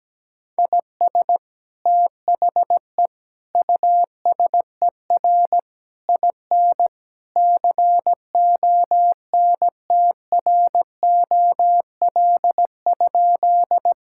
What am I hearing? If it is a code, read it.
Morse code, heard as a keyed tone, about 17 wpm, 700 Hz. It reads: IS THE USER IN CONTROL?
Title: Sentences from Top 1000 Words - Review Part 5 - 17wpm